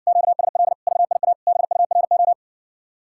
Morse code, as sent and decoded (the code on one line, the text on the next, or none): --.- ... .-.. / ...- .. .- / -... ..- .-. ---
QSL VIA BURO